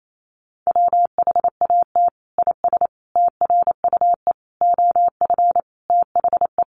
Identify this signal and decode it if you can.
Morse code, heard as a keyed tone, about 28 wpm, 700 Hz. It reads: WHAT IS TRUE OF THE
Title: Sentences from Top 1400 Words 28wpm